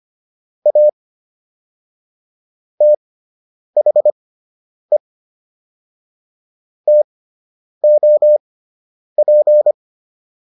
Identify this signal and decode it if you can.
Morse code, heard as a keyed tone, about 25 wpm, spaced as for 9 wpm, 600 Hz. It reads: A THE TOP